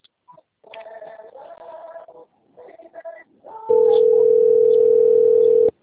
Ringback tone (North America), 440 + 480 Hz.
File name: out- - - - - .32272